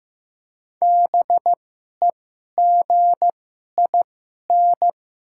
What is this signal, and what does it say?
Morse code, heard as a keyed tone, about 15 wpm, spaced as for 11 wpm, 700 Hz. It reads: BEGIN
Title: 2x Character Spacing - Top 300 Words - 15wpm